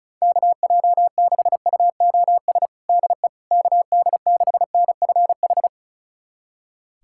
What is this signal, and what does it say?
Morse code, heard as a keyed tone, about 36 wpm, 700 Hz. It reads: KJ6UOS DE KD6NFH
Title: USA Callsign DE Callsign - 6 Characters - 35wpm